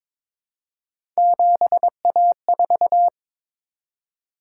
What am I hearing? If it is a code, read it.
Morse code, heard as a keyed tone, about 22 wpm, 700 Hz. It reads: 7A4
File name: practice / 22wpm / Three Letter-Number Combinations - 22wpm